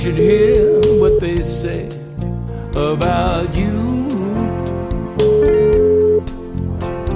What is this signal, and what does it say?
Ringback tone (Europe, 425 Hz), with noise.